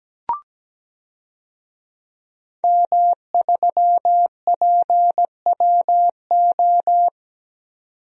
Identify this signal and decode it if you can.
Morse code, heard as a keyed tone, about 17 wpm, 700 Hz. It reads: M3PWO
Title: DXCC-100 Call Signs - 5 Characters - 17wpm